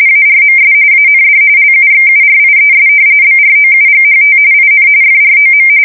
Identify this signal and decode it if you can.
RTTY, about 45.45 baud. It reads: ARMS AND LUXURIES BROUGHT IN FROM
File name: fsk170 / usnavy3